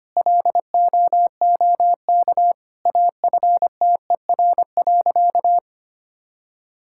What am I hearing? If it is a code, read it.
Morse code, heard as a keyed tone, about 25 wpm, 700 Hz. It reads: LOOK AFTER.
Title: Sentences from Top 1600 Words 25wpm